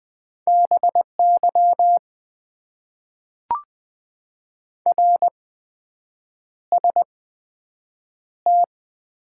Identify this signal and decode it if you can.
Morse code, heard as a keyed tone, about 20 wpm, spaced as for 7 wpm, 700 Hz. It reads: BY RST
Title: Top 90 QSO Elements - 8x Character Spacing - 20wpm